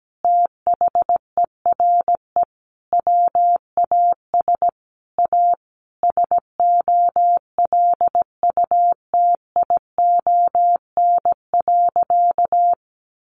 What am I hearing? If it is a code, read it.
Morse code, heard as a keyed tone, about 17 wpm, 700 Hz. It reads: THERE WAS A SOLUTION.
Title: Sentences from Top 1300 Words 17wpm